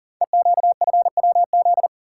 Morse code, heard as a keyed tone, about 38 wpm, 700 Hz. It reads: EQFWZ
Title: Code Groups - 5 Letter:Numbers - 40wpm